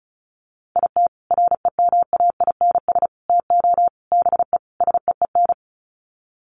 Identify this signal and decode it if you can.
Morse code, heard as a keyed tone, about 34 wpm, 700 Hz. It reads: IT REMAINS TO BE SEEN
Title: Sets of 5 Words Encore 35wpm